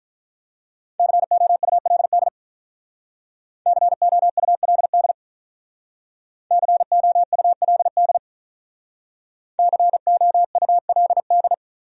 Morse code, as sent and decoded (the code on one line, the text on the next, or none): -.-. --- ..- .-.. -.. / -.-. --- ..- .-.. -.. / -.-. --- ..- .-.. -.. / -.-. --- ..- .-.. -..
COULD COULD COULD COULD